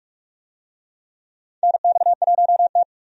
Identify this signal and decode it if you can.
Morse code, heard as a keyed tone, about 46 wpm, 700 Hz. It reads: NX1T
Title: USA Call Signs - 4 Characters - 45wpm